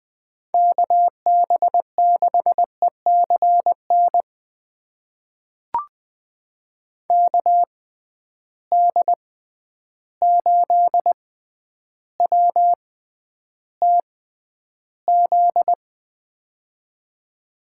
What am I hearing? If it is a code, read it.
Morse code, heard as a keyed tone, about 20 wpm, spaced as for 10 wpm, 700 Hz. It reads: KB6ECN KD8WTZ
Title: USA Call Signs - 6 Characters - 6x Character Spacing - 20wpm